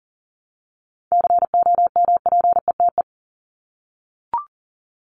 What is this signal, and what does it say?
Morse code, heard as a keyed tone, about 40 wpm, 700 Hz. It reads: COMPETE
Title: Top 1751-2000 Words 40wpm